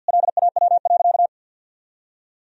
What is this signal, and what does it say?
Morse code, heard as a keyed tone, about 50 wpm, 700 Hz. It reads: LAW.
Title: Sentences from Top 1000 Words - Review Part 4 - 50wpm